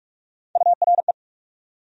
Morse code, heard as a keyed tone, about 46 wpm, 700 Hz. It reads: URE
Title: USA Call Sign Postfixes - 3 Letters 45wpm